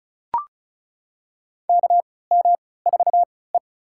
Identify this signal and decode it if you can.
Morse code, heard as a keyed tone, about 34 wpm, spaced as for 20 wpm, 700 Hz. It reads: KM4E
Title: USA Call Signs - 3x Character Spacing - 35wpm